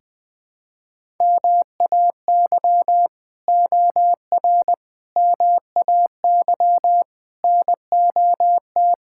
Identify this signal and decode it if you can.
Morse code, heard as a keyed tone, about 20 wpm, 700 Hz. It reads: MAY OR MAY NOT
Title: Build A Phrase - 5-Word Phrases - 20wpm